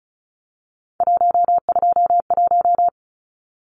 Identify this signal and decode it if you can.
Morse code, heard as a keyed tone, about 34 wpm, 700 Hz. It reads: 121
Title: Numbers - 3-Digit 35wpm